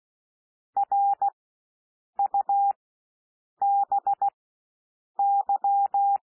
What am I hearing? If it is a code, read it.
Morse code, heard as a keyed tone, about 16 wpm, spaced as for 7 wpm, 800 Hz. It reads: RUBY
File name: SOUND WORDS 16 wpm